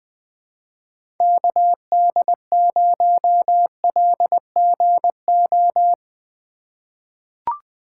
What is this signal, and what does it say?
Morse code, heard as a keyed tone, about 21 wpm, 700 Hz. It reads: KD0LGO
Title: USA Call Signs - 6 Characters - 4x Character Spacing - 20wpm